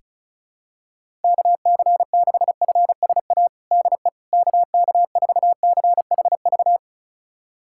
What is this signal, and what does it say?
Morse code, heard as a keyed tone, about 36 wpm, 700 Hz. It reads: KC6FSA DE KK4CHV
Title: USA Callsign DE Callsign 35wpm